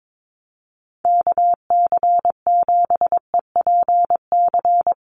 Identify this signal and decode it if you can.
Morse code, heard as a keyed tone, about 22 wpm, 700 Hz. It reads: KC7EPC
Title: USA Call Signs Repeated 3x 22wpm